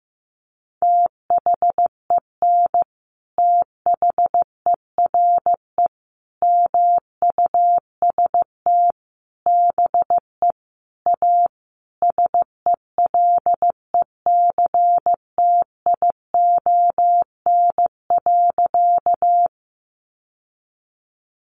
Morse code, heard as a keyed tone, about 15 wpm, 700 Hz. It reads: THEN THERE MUST BE A SELECTION.
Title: Sentences from Top 1400 Words 15wpm